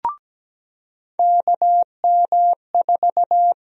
Morse code, heard as a keyed tone, about 17 wpm, 700 Hz. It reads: KM4
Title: USA Call Sign Prefixes - 17wpm